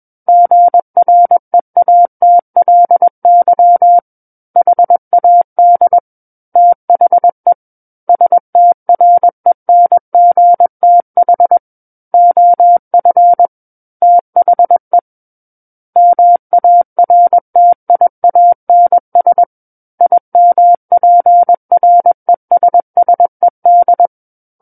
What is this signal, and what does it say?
Morse code, heard as a keyed tone, about 21 wpm, 700 Hz. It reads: GREATLY HAD THE STRENGTH OF THE MARTIANS IMPRESSED
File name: Book One / 12-Chapter 12 - 21 WPM